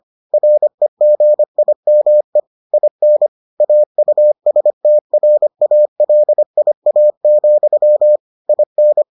Morse code, heard as a keyed tone, about 25 wpm, 600 Hz. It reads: REGIME IN AUSTRALIA, IN